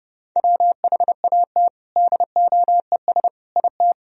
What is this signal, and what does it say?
Morse code, heard as a keyed tone, about 30 wpm, 700 Hz. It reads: WHAT DOES IT
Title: Sentences from Top 1000 Words - Review Part 2 - 30wpm